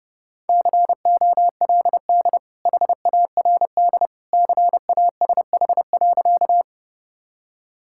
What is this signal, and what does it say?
Morse code, heard as a keyed tone, about 32 wpm, 700 Hz. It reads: COLD HARD CASH.
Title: Sentences from Top 1000 Words - Review Part 5 - 30wpm